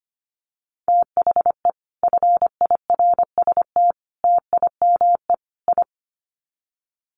Morse code, heard as a keyed tone, about 25 wpm, 700 Hz. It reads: THE FIRST TIME I
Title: Sets of 4 Words 25wpm